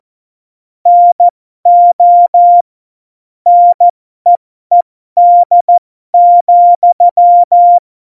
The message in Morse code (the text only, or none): NO NEED,